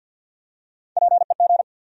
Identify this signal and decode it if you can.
Morse code, heard as a keyed tone, about 53 wpm, 700 Hz. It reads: PEG
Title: USA Common Names - Max 5 Letters - 50wpm